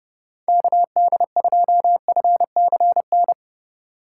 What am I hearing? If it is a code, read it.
Morse code, heard as a keyed tone, about 32 wpm, 700 Hz. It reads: KD2FCN
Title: USA Call Signs - 6 Characters - 8x Character Spacing - 30wpm